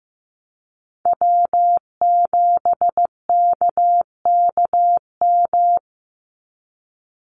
Morse code, heard as a keyed tone, about 15 wpm, 700 Hz. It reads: W7KKM
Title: NA SOTA Chaser Callsigns 15wpm